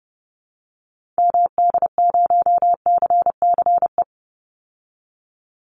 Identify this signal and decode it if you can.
Morse code, heard as a keyed tone, about 30 wpm, 700 Hz. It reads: MD0CCE
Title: Call Signs Repeated 2x - 30wpm